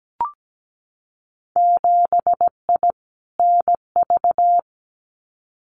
Morse code, heard as a keyed tone, about 17 wpm, 700 Hz. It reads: 7I NV
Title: Winter Field Day Exchanges Encore 17wpm